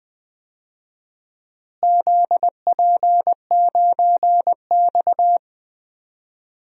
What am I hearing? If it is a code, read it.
Morse code, heard as a keyed tone, about 20 wpm, 700 Hz. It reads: ZP9X
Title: DXCC-100 Call Signs - 4 Characters - Repeated 3x - 20wpm